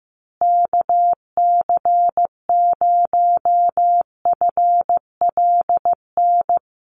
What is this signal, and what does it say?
Morse code, heard as a keyed tone, about 15 wpm, 700 Hz. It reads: KC0FLN
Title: USA Call Signs Repeated 3x 15wpm